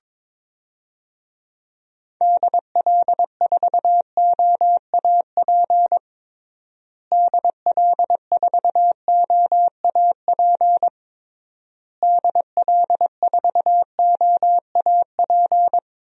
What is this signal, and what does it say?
Morse code, heard as a keyed tone, about 22 wpm, 700 Hz. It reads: DL4OAP DL4OAP DL4OAP
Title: USA Call Signs - 6 Characters - Repeated 3x - 22wpm